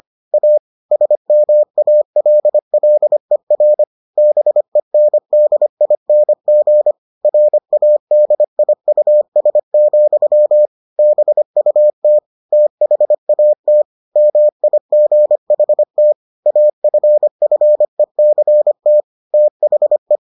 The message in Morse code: A SMALLER BENDING RADIUS, BUT THAT MIGHT AFFECT THE

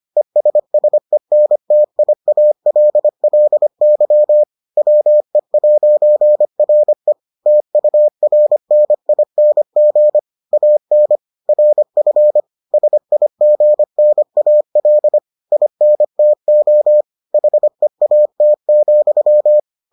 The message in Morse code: ESSENTIALLY WE'RE TURNING AN RF SIGNAL INTO HEAT,

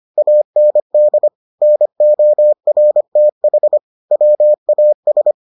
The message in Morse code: AND NORTH WAS

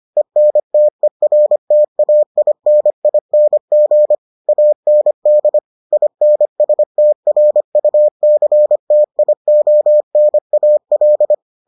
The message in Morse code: ENTERTAINING AND INSTRUCTIONAL

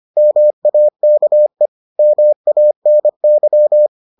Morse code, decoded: MAKE MANY

